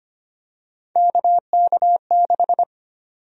Morse code, heard as a keyed tone, about 25 wpm, 700 Hz. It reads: KK6
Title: USA Call Sign Prefixes - 25wpm